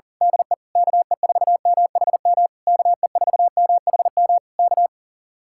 Morse code, heard as a keyed tone, about 37 wpm, 700 Hz. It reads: DE KE4MHM KE4MHM K
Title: Speed-Racing Common QSO Phrases Encore - 40wpm